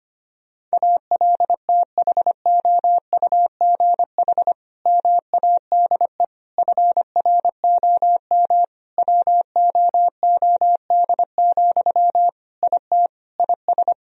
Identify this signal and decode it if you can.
Morse code, heard as a keyed tone, about 25 wpm, 700 Hz. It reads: ALTHOUGH MADE FROM WOOD, IT IS
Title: Sentences from Top 1000 Words - Review Part 3 - 25wpm